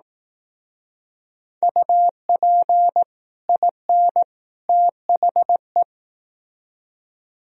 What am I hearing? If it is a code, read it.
Morse code, heard as a keyed tone, about 18 wpm, 700 Hz. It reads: UP IN THE
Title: Speed-Racing - Sets of 3 Words 15wpm